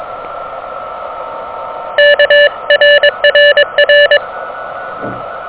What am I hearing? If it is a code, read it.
Morse code, heard as a keyed tone, about 21 wpm, 590 Hz, with noise. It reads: KRRR
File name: OK QRP 2023 / OK QRP 2023